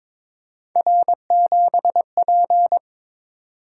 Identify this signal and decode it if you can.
Morse code, heard as a keyed tone, about 23 wpm, 700 Hz. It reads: R7P
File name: practice / 22wpm / Three Letter-Number Combinations - 22wpm